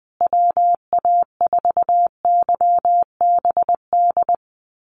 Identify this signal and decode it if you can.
Morse code, heard as a keyed tone, about 20 wpm, 700 Hz. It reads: WA4YBD
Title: USA Call Signs Repeated 3x 20wpm